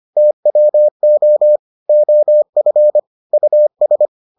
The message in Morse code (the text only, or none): TWO OF US